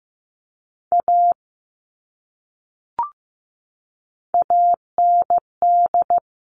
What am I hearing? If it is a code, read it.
Morse code, heard as a keyed tone, about 15 wpm, 700 Hz. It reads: A AND
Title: Top 500 Words 15wpm